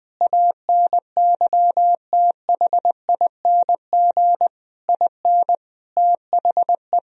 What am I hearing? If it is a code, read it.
Morse code, heard as a keyed tone, about 20 wpm, 700 Hz. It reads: ANYTHING IN THE
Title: Sentences from Top 1000 Words - Review Part 2 - 20wpm